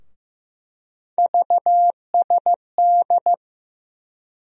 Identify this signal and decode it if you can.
Morse code, heard as a keyed tone, about 15 wpm, 700 Hz. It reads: VSD
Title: USA Call Sign Postfixes - 3 Letters 15wpm